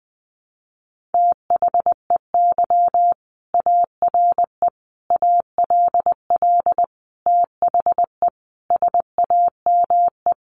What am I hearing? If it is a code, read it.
Morse code, heard as a keyed tone, about 20 wpm, 700 Hz. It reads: THEY ARE ALL THE SAME
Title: Sentences from Top 600 Words 20wpm